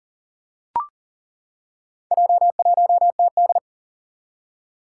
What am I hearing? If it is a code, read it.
Morse code, heard as a keyed tone, about 37 wpm, 700 Hz. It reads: J1TD
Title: Four Challenging Letter-Number Combinations 40wpm